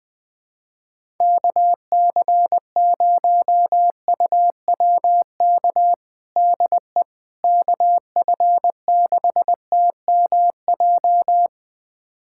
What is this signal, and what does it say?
Morse code, heard as a keyed tone, about 20 wpm, 700 Hz. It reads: KC0UWK DE KF6TMJ
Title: USA Callsign DE Callsign - 6 Characters - 20wpm